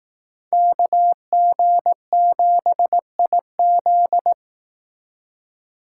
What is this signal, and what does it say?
Morse code, heard as a keyed tone, about 18 wpm, 700 Hz. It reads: KG7IZ
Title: Speed Racing - USA 2x2 Call Signs 15wpm